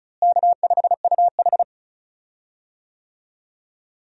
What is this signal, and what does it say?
Morse code, heard as a keyed tone, about 34 wpm, 700 Hz. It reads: K5UH
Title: USA Call Signs - 4 Characters - Repeated 2x - 35wpm